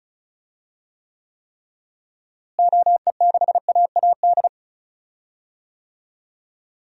Morse code, heard as a keyed tone, about 36 wpm, 700 Hz. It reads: OE6AAD
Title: DXCC-100 Call Signs - 6 Characters - Repeated 3x - 35wpm